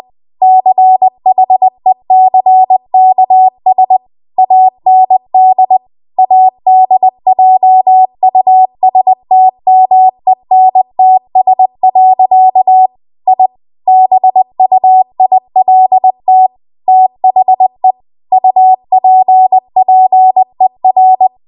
Morse code, heard as a keyed tone, about 20 wpm, 750 Hz. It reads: CHECKS AND ADJUSTMENTS. I BUILT THE UPPER